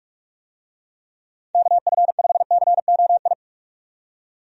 Morse code, heard as a keyed tone, about 42 wpm, 700 Hz. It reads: KF5COI
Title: USA Call Signs - 6 Characters - 45wpm